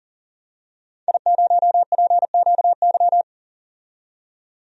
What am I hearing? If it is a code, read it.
Morse code, heard as a keyed tone, about 40 wpm, 700 Hz. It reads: I0PQY
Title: DXCC-100 Call Signs - 5 Characters - 40wpm